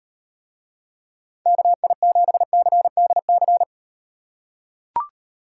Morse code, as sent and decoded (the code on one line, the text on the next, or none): -.- .. --... -.-. -.. -.-.
KI7CDC